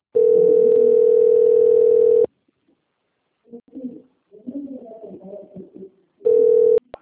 Ringback tone (North America), 440 + 480 Hz.